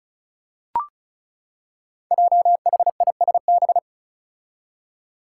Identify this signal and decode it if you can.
Morse code, heard as a keyed tone, about 34 wpm, 700 Hz. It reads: JHISB
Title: Five Challenging Letter-Number Combinations 35wpm